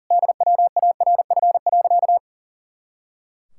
Morse code, heard as a keyed tone, about 40 wpm, 700 Hz. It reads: DWARF.
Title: Pangram Challenge v2 - 40wpm